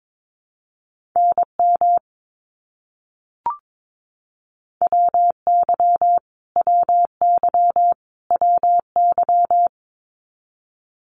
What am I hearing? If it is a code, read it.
Morse code, heard as a keyed tone, about 22 wpm, 700 Hz. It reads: NM WY WY WY
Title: US States Repeated 3x 22wpm